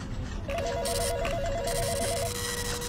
There is no code.